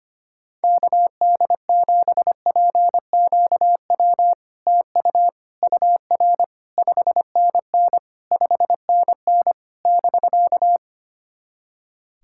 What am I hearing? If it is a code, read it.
Morse code, heard as a keyed tone, about 25 wpm, 700 Hz. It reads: KD7PQW TU UR 5NN 5NN <BK>
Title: SOTA QSO Practice 25wpm